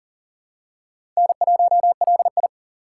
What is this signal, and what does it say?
Morse code, heard as a keyed tone, about 40 wpm, 700 Hz. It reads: N1LI
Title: USA Call Signs - 4 Characters - 8x Character Spacing - 40wpm